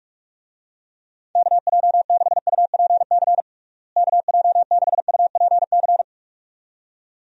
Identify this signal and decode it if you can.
Morse code, heard as a keyed tone, about 43 wpm, 700 Hz. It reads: KJ6UPC KJ6UPC
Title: USA Call Signs - 6 Characters - Repeated 2x - 45wpm